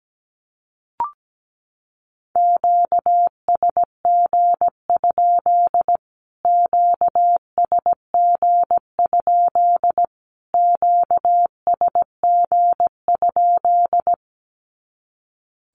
Morse code, heard as a keyed tone, about 17 wpm, 700 Hz. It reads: QSG? QSG? QSG?
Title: Q-Codes Repeated 3x 17wpm